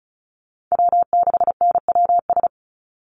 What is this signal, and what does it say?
Morse code, heard as a keyed tone, about 34 wpm, 700 Hz. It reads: W6NWS
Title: Call Signs - 35wpm